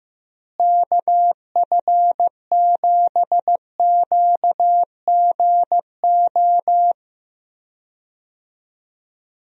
Morse code, heard as a keyed tone, about 15 wpm, 700 Hz. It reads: KF7QGO
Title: USA Call Signs - 6 Characters - Repeated 2x - 15wpm